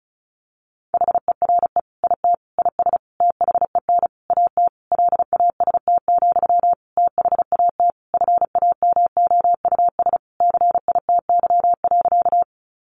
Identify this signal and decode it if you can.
Morse code, heard as a keyed tone, about 34 wpm, 700 Hz. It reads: HERE IT IS THEN AT LAST, THAT FAMOUS CITY.
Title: Sentences from Top 1500 Words 35wpm